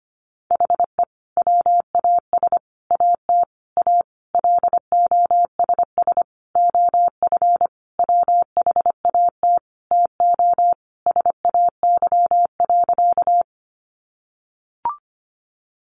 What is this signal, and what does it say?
Morse code, heard as a keyed tone, about 25 wpm, 700 Hz. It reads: HE WAS AT A LOSS OF WHAT TO SAY.
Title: Sentences from Top 1400 Words 25wpm